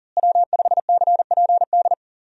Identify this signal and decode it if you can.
Morse code, heard as a keyed tone, about 40 wpm, 700 Hz. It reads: W5CPD
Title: USA Call Signs - 5 Characters - Repeated 3x - 40wpm